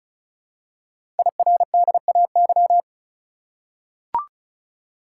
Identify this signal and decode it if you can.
Morse code, heard as a keyed tone, about 36 wpm, 700 Hz. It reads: IRDAY
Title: Code Groups - 5 Letters - 35wpm